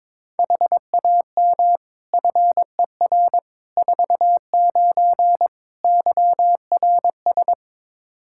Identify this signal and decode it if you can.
Morse code, heard as a keyed tone, about 21 wpm, 700 Hz. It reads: HAM FER 49 YRS
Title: Practice QSO - Exchange 3 - More Info - 22wpm